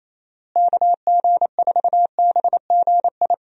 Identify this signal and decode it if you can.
Morse code, heard as a keyed tone, about 28 wpm, 700 Hz. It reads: KG4BGI
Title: USA Call Signs - 6 Characters - Repeated 2x - 28wpm